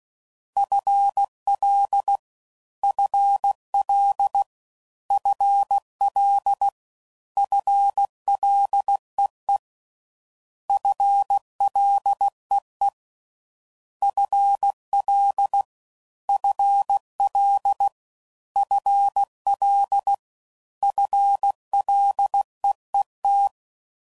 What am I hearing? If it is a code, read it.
Morse code, heard as a keyed tone, about 16 wpm, 800 Hz. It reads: FL FL FL FLEE FLEE FL FL FL FLEET
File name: Prefixes / FLs & THs / FL Prefixes 3